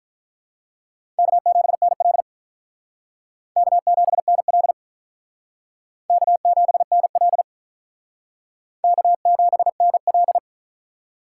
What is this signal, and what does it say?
Morse code, heard as a keyed tone, about 44 wpm, 700 Hz. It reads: K7NL K7NL K7NL K7NL